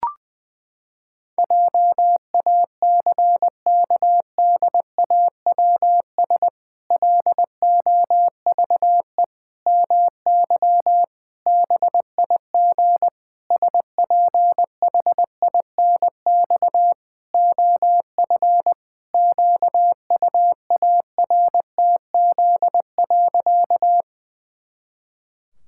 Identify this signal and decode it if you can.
Morse code, heard as a keyed tone, about 20 wpm, 700 Hz. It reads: JACKDAWS LOVE MY BIG SPHINX OF QUARTZ.
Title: Pangram Challenge v2 - 20wpm